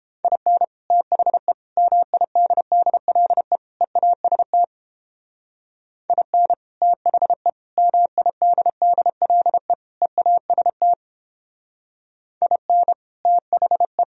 Morse code, heard as a keyed tone, about 29 wpm, 700 Hz. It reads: IN THE MIDDLE EAST IN THE MIDDLE EAST IN THE